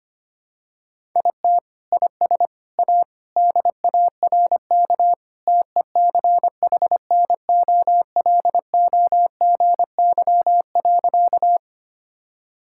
Morse code, heard as a keyed tone, about 25 wpm, 700 Hz. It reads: IT IS A DARK TECHNOLOGY.
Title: Sentences from Top 1000 Words - Review Part 5 - 25wpm